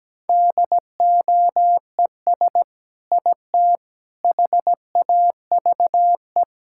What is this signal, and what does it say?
Morse code, heard as a keyed tone, about 17 wpm, 700 Hz. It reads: DOES IT HAVE
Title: Sentences from Top 1000 Words - Review Part 5 - 17wpm